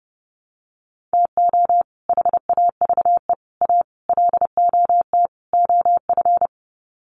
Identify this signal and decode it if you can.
Morse code, heard as a keyed tone, about 30 wpm, 700 Hz. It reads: TO HAVE A LOT OF
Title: Sets of 5 Words Encore 30wpm